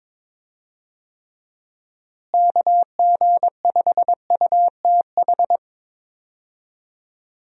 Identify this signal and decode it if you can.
Morse code, heard as a keyed tone, about 22 wpm, 700 Hz. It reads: KG5UTH
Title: USA Call Signs - 6 Characters - Repeated 3x - 22wpm